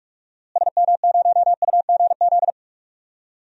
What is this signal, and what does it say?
Morse code, heard as a keyed tone, about 41 wpm, 700 Hz. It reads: SM0UGZ
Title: DXCC-100 Call Signs - 6 Characters - Repeated 3x - 45wpm